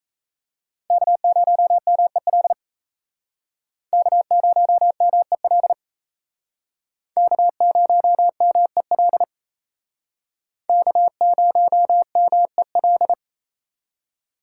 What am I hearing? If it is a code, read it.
Morse code, heard as a keyed tone, about 35 wpm, 700 Hz. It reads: K0MEL K0MEL K0MEL K0MEL